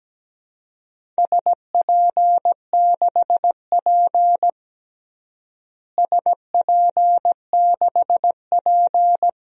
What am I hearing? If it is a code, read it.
Morse code, heard as a keyed tone, about 17 wpm, 700 Hz. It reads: SP6P SP6P
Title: DXCC-100 Call Signs - 4 Characters - Repeated 2x - 17wpm